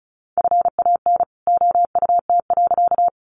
Morse code, heard as a keyed tone, about 34 wpm, 700 Hz. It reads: FAN OUT.
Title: Sentences from Top 1300 Words 35wpm